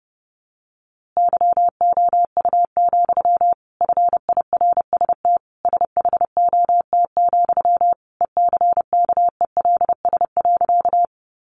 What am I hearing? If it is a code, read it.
Morse code, heard as a keyed tone, about 30 wpm, 700 Hz. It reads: YOU, FIRST SHOT, ECKELS.
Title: A Sound Of Thunder 30wpm